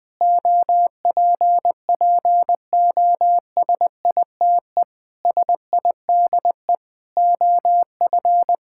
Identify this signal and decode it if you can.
Morse code, heard as a keyed tone, about 19 wpm, 700 Hz. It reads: OPPOSITE SIDE OF
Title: Sentences with 6-Word Phrases - Spoken Beginning - 20wpm